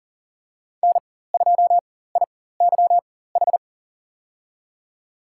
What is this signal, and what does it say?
Morse code, heard as a keyed tone, about 40 wpm, spaced as for 19 wpm, 700 Hz. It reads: N2IYH
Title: USA Call Signs - 5 Characters - 4x Character Spacing - 40wpm